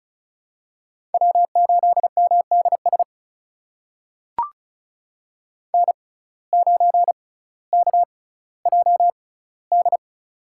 Morse code, heard as a keyed tone, about 36 wpm, spaced as for 18 wpm, 700 Hz. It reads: W8MDS N9KJD
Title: USA Call Signs - 5 Characters - 6x Character Spacing - 35wpm